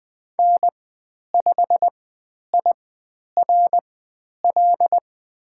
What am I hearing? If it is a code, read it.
Morse code, heard as a keyed tone, about 20 wpm, spaced as for 10 wpm, 700 Hz. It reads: N5IRL
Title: 102 Lesson 17g - Letter u - Call Signs 20wpm